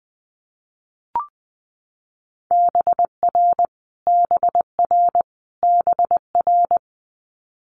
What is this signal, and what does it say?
Morse code, heard as a keyed tone, about 20 wpm, 700 Hz. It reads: BR BR BR
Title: Multi-letter Phonemes Repeated 3x 20wpm